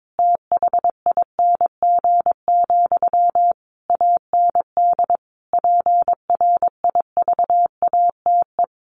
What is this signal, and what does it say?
Morse code, heard as a keyed tone, about 22 wpm, 700 Hz. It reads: THING, AND PRIVATE
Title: Sentences from Top 1000 Words 22wpm